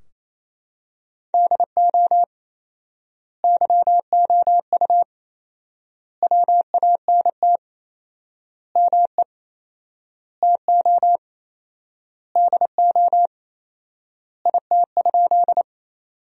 Morse code, heard as a keyed tone, about 28 wpm, 700 Hz. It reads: DO YOU WANT ME TO DO IT?
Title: 4x Word-Spacing - Sentences from Top 200 Words - 28wpm